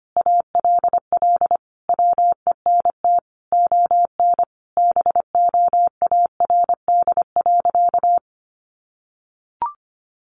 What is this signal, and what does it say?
Morse code, heard as a keyed tone, about 25 wpm, 700 Hz. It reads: ALL WENT ON BOARD.
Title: Sentences from Top 1400 Words 25wpm